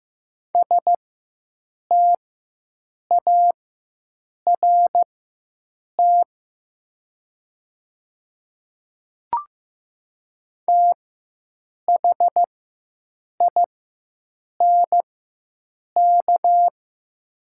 Morse code, heard as a keyed tone, about 15 wpm, spaced as for 7 wpm, 700 Hz. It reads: START THINK
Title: 4x Character Spacing - Top 300 Words - 15wpm